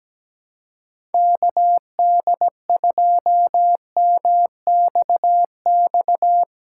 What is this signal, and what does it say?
Morse code, heard as a keyed tone, about 17 wpm, 700 Hz. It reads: KD2MXX
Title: USA Callsign DE Callsign 17wpm